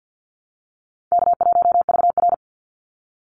Morse code, heard as a keyed tone, about 48 wpm, 700 Hz. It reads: K1VR
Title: Easy Call Signs Repeated 2x - 50wpm